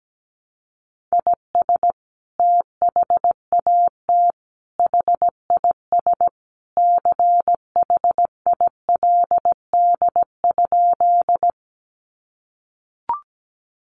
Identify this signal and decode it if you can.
Morse code, heard as a keyed tone, about 17 wpm, 700 Hz. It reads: IS THAT HIS CHILD?
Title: Sentences from Top 800 Words 17wpm